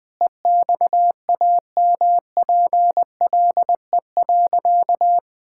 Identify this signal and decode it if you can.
Morse code, heard as a keyed tone, about 20 wpm, 700 Hz. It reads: EXAMPLE.
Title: Sentences from Top 1000 Words - Review Part 5 - 20wpm